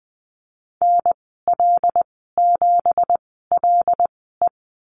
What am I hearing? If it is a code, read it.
Morse code, heard as a keyed tone, about 20 wpm, spaced as for 15 wpm, 700 Hz. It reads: NL7LE
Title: 213 Lesson 34g - Number 8 - Call Signs 20wpm